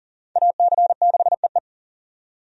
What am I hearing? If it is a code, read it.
Morse code, heard as a keyed tone, about 40 wpm, 700 Hz. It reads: AC6EE
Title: USA Call Signs - 5 Characters - 40wpm